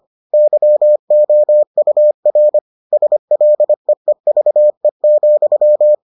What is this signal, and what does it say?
Morse code, heard as a keyed tone, about 25 wpm, 600 Hz. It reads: YOUR SLEEVE,